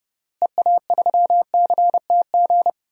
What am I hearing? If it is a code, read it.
Morse code, heard as a keyed tone, about 30 wpm, 700 Hz. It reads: EA3CTG
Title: DXCC-100 Call Signs - 6 Characters - Repeated 3x - 30wpm